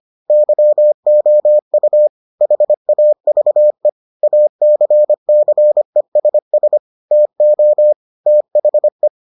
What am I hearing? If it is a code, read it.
Morse code, heard as a keyed tone, about 25 wpm, 600 Hz. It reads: YOU HAVE ACCESS TO THE